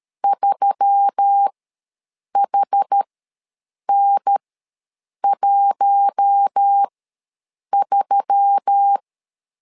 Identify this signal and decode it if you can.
Morse code, heard as a keyed tone, about 13 wpm, spaced as for 7 wpm, 800 Hz. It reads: 3HN13
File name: LEZIONI / 4 Q1S3 ws 12wpm 2038